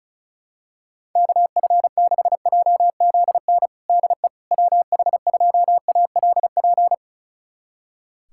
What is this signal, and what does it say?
Morse code, heard as a keyed tone, about 34 wpm, 700 Hz. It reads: KF6JZN DE WH2ALP
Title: USA Callsign DE Callsign - 6 Characters - 35wpm